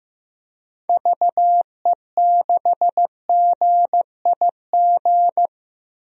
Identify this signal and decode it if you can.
Morse code, heard as a keyed tone, about 15 wpm, 700 Hz. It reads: VE6GIG
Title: DXCC-100 Call Signs - 6 Characters - Repeated 3x - 15wpm